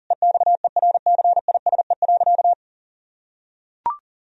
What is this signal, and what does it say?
Morse code, heard as a keyed tone, about 40 wpm, 700 Hz. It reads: EXERCISE.
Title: Sentences from Top 1000 Words - Review Part 4 - 40wpm